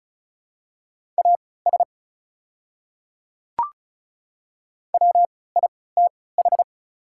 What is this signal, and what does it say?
Morse code, heard as a keyed tone, about 36 wpm, spaced as for 20 wpm, 700 Hz. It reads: AS WITH
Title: 3x Character Spacing - Top 100 Words - 35wpm